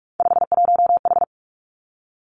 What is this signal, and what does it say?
Morse code, heard as a keyed tone, about 46 wpm, 700 Hz. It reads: 51H
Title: Three Challenging Letter-Number Combinations 45wpm